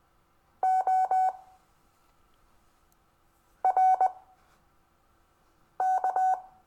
Morse code, heard as a keyed tone, about 20 wpm, spaced as for 5 wpm, 750 Hz. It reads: O RK
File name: cw / Lesson22 REVIEW